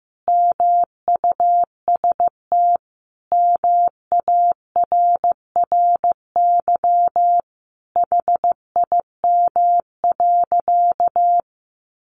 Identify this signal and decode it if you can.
Morse code, heard as a keyed tone, about 14 wpm, 700 Hz. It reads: MUST MARRY HIM.
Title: Sentences from Top 1200 Words 15wpm